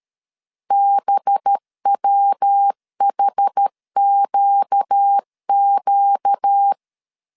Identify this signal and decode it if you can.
Morse code, heard as a keyed tone, about 13 wpm, 800 Hz. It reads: BWHQQ
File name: CW / DET / 1 6 1500C S B 12wpm 3431